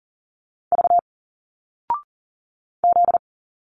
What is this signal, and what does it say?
Morse code, heard as a keyed tone, about 40 wpm, 700 Hz. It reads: V Z
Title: Single Letters 40wpm